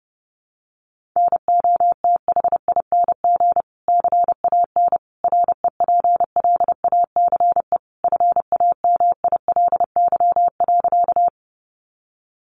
Morse code, heard as a keyed tone, about 30 wpm, 700 Hz. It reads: NOTHING CAN REPLACE FAMILY.
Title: Sentences from Top 1400 Words 30wpm